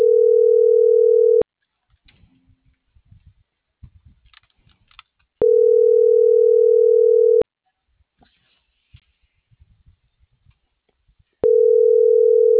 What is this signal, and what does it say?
Ringback tone (North America), 440 + 480 Hz.